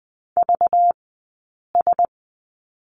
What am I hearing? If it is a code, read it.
Morse code, heard as a keyed tone, about 20 wpm, spaced as for 8 wpm, 700 Hz. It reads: VS
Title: 217 Lesson 35d - Number 0 - Two Characters 20wpm